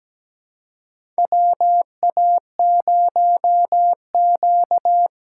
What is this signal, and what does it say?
Morse code, heard as a keyed tone, about 17 wpm, 700 Hz. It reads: WA0Q
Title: USA Call Signs - 4 Characters - 2x Character Spacing - 17wpm